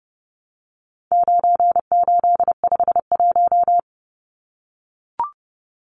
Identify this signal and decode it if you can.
Morse code, heard as a keyed tone, about 30 wpm, 700 Hz. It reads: 9851
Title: Numbers - 4-Digit 30wpm